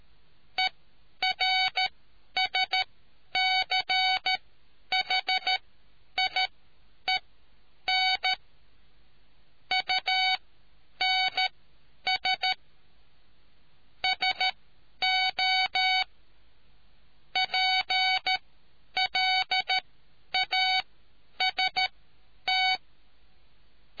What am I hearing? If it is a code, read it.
Morse code, heard as a keyed tone, about 13 wpm, spaced as for 9 wpm, 750 Hz. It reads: ERSCHIEN UNS SO PLAST